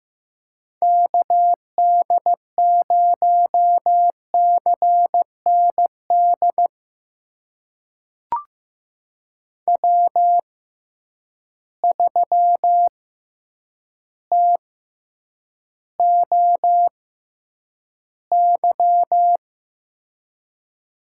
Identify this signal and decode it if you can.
Morse code, heard as a keyed tone, about 15 wpm, spaced as for 8 wpm, 700 Hz. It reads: KD0CND W3TOY
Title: USA Call Signs - 6x Character Spacing - 15wpm